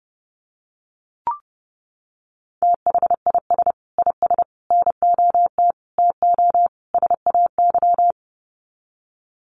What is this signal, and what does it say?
Morse code, heard as a keyed tone, about 30 wpm, 700 Hz. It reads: THIS IS NOT TO SAY
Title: Sets of 5 Words Encore 30wpm